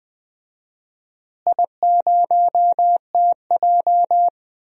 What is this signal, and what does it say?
Morse code, heard as a keyed tone, about 20 wpm, 700 Hz. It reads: I0TJ
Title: DXCC-100 Call Signs - 4 Characters - Repeated 3x - 20wpm